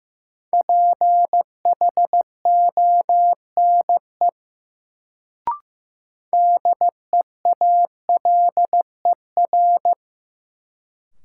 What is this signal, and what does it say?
Morse code, heard as a keyed tone, about 16 wpm, 700 Hz. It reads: PHONE DEALER
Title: Sentences from Top 1100 Words - New Word Drill Alt - 15wpm